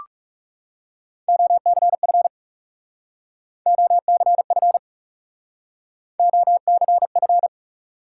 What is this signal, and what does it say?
Morse code, heard as a keyed tone, about 37 wpm, 700 Hz. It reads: OCF OCF OCF
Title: Speed-Racing Top 180 QSO Elements 30wpm